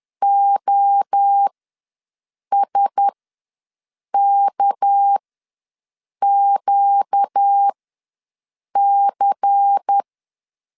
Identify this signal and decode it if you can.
Morse code, heard as a keyed tone, about 10 wpm, spaced as for 6 wpm, 800 Hz. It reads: OSKQC
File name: LEZIONI / 9 286 ws 10wpm 4032